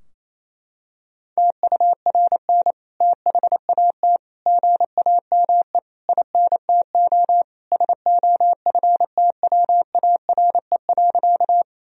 Morse code, heard as a keyed tone, about 28 wpm, 700 Hz. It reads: TURN THAT GAME INTO SOFTWARE.
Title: Sentences from Top 1000 Words - Review Part 3 - 28wpm